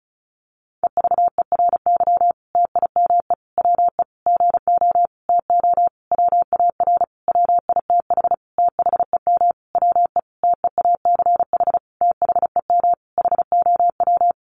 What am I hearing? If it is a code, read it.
Morse code, heard as a keyed tone, about 34 wpm, 700 Hz. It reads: EVERY TIME WE GO TO WAR WITH THEM WE TEACH THEM HOW
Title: Sentences from Top 1300 Words 35wpm